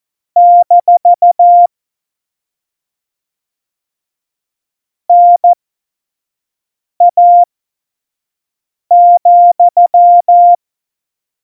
Morse code, heard as a keyed tone, about 14 wpm, spaced as for 5 wpm, 700 Hz. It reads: - NA,
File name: Morsekurs / Witze / Witz 13. Oktober 2020, 25 F70